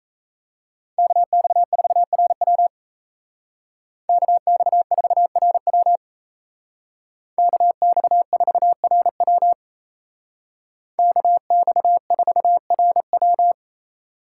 Morse code, heard as a keyed tone, about 36 wpm, 700 Hz. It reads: KX4RW KX4RW KX4RW KX4RW